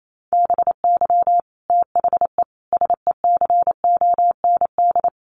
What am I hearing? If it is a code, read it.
Morse code, heard as a keyed tone, about 28 wpm, 700 Hz. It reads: BY THE SECOND
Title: Sentences from Top 1200 Words 28wpm